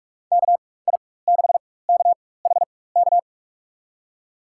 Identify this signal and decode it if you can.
Morse code, heard as a keyed tone, about 43 wpm, spaced as for 21 wpm, 700 Hz. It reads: KI6KHK
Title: USA Call Signs - 6 Characters - 4x Character Spacing - 45wpm